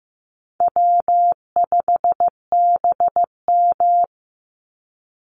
Morse code, heard as a keyed tone, about 15 wpm, 700 Hz. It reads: W5BM
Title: Easy Call Signs Repeated 3x - 15wpm